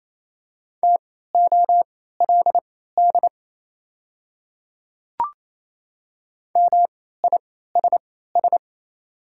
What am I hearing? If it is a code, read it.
Morse code, heard as a keyed tone, about 27 wpm, spaced as for 16 wpm, 700 Hz. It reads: TOLD MISS
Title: 3x Character Spacing - Top 500 Words - 28wpm